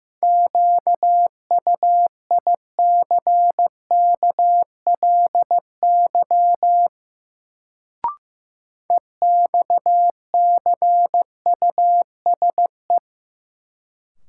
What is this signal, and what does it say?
Morse code, heard as a keyed tone, about 15 wpm, 700 Hz. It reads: QUICKLY EXCUSE
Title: Sentences from Top 700 Words - New Word Drill - 15wpm